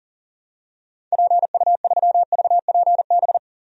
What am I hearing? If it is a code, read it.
Morse code, heard as a keyed tone, about 40 wpm, 700 Hz. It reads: PU3VPB
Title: DXCC-100 Call Signs - 6 Characters - 40wpm